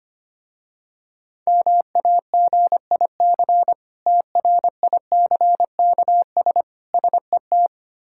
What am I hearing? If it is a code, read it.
Morse code, heard as a keyed tone, about 25 wpm, 700 Hz. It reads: MAGIC TRICKS SET
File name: practice / 25wpm / Difficult Words with Hints - 25wpm